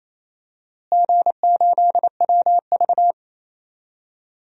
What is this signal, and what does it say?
Morse code, heard as a keyed tone, about 27 wpm, 700 Hz. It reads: G8WV
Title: DXCC-100 Call Signs - 4 Characters - Repeated 3x - 28wpm